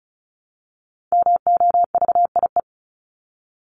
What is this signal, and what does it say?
Morse code, heard as a keyed tone, about 34 wpm, 700 Hz. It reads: MOVIE
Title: Top 501-750 Words 35wpm